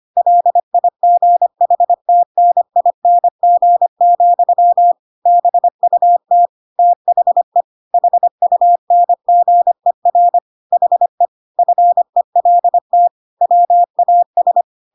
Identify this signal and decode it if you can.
Morse code, heard as a keyed tone, about 25 wpm, 700 Hz. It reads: LIGHTNING, BUT THE HUNGER HE FELT WAS
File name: Pinocchio / Pinocchio ch6 25 25 ik1hge